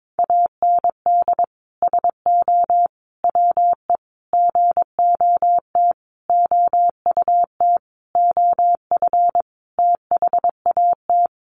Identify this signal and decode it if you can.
Morse code, heard as a keyed tone, about 22 wpm, 700 Hz. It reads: AND SO WE GOT OUT OF THAT
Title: Sentences from Top 1500 Words 22wpm